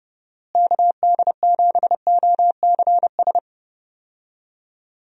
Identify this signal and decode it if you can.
Morse code, heard as a keyed tone, about 30 wpm, 700 Hz. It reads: KD7OCS
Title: USA Call Signs - 6 Characters - Repeated 3x - 30wpm